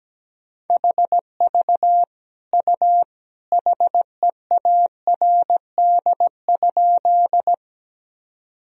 Morse code, heard as a keyed tone, about 17 wpm, 700 Hz. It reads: HV U HEARD?
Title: Sentences from Top 400 Words and Abbreviations 17wpm